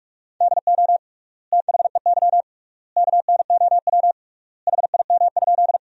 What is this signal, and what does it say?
Morse code, heard as a keyed tone, about 47 wpm, 700 Hz. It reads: DO THEY KNOW HIM?
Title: 3x Word-Spacing - Sentences from Top 100 Words - 45wpm